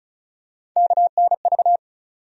Morse code, heard as a keyed tone, about 34 wpm, 700 Hz. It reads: KNV
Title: USA Call Sign Postfixes - 3 Letters 35wpm